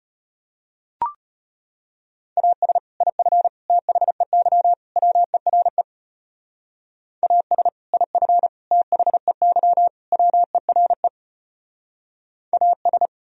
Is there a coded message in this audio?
Yes: Morse code from 2.4 to 13.1 s, heard as a keyed tone, about 36 wpm, 700 Hz. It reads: AS IF THEY WERE AS IF THEY WERE AS